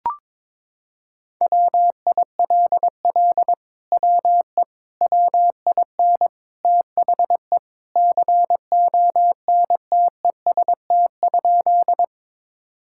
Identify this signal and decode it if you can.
Morse code, heard as a keyed tone, about 22 wpm, 700 Hz. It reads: WILL WE WIN THE CONTEST?
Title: Sentences from Top 1600 Words 22wpm